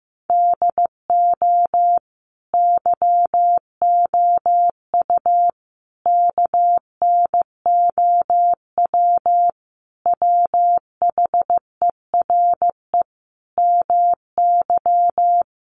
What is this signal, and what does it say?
Morse code, heard as a keyed tone, about 15 wpm, 700 Hz. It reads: DO YOU KNOW WHERE MY
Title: Sentences from Top 1400 Words 15wpm